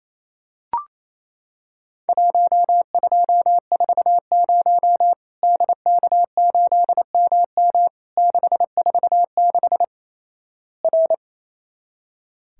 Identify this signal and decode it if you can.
Morse code, heard as a keyed tone, about 28 wpm, 700 Hz. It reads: 1240 DK8MM 646 R
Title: Worked All Europe DX Contest - QTC Exchange - 28wpm